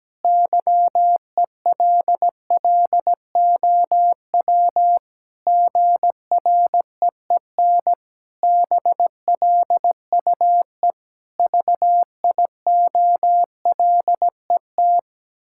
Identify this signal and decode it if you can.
Morse code, heard as a keyed tone, about 17 wpm, 700 Hz. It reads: YELLOW GREEN BLUE VIOLET
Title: Pangram Challenge v2 - 17wpm